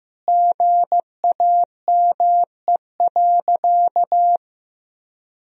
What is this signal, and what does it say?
Morse code, heard as a keyed tone, about 16 wpm, 700 Hz. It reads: GAME.
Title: Sentences from Top 1000 Words - Review Part 4 - 15wpm